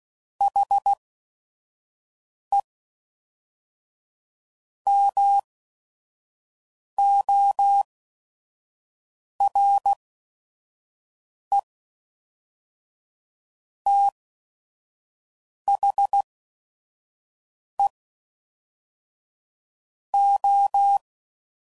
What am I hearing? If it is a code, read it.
Morse code, heard as a keyed tone, about 16 wpm, 800 Hz. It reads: H E M O R E T H E O